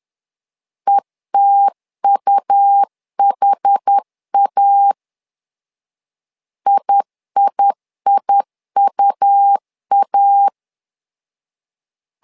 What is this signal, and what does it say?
Morse code, heard as a keyed tone, about 10 wpm, 800 Hz. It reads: ETUHA IIIUA
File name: LEZIONI / 2 UHDN / 2 UHD 10wpm 1826